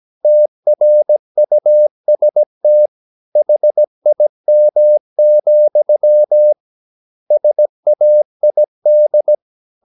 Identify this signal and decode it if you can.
Morse code, heard as a keyed tone, about 17 wpm, 600 Hz. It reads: TRUST HIM, SAID
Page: Romanen Never af Ken Follet på engelsk.